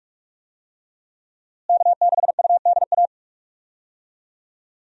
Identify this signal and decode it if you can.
Morse code, heard as a keyed tone, about 45 wpm, 700 Hz. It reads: K6UDA
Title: USA Call Signs - 5 Characters - Repeated 2x - 45wpm